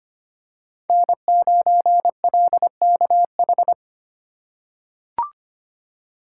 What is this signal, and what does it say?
Morse code, heard as a keyed tone, about 25 wpm, 700 Hz. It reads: N9LKH
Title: USA Call Signs - 5 Characters - 2x Character Spacing - 25wpm